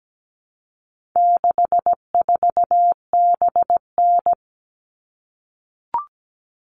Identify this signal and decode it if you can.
Morse code, heard as a keyed tone, about 17 wpm, 700 Hz. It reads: 64BN
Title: All Cut Numbers - 4-Digit 17wpm